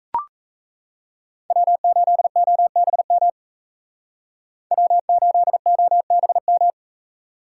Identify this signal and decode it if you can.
Morse code, heard as a keyed tone, about 37 wpm, 700 Hz. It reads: W8OBM W8OBM